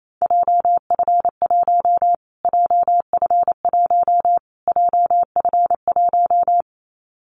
Morse code, heard as a keyed tone, about 28 wpm, 700 Hz. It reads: JF1 JF1 JF1
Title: Callsign Prefixes for Top 25 DXCC - Repeated 3x - 28wpm